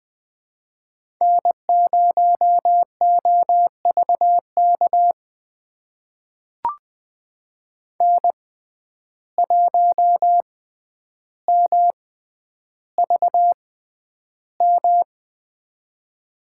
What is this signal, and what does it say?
Morse code, heard as a keyed tone, about 20 wpm, spaced as for 10 wpm, 700 Hz. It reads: N0OVK N1MVM
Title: USA Call Signs - 5 Characters - 6x Character Spacing - 20wpm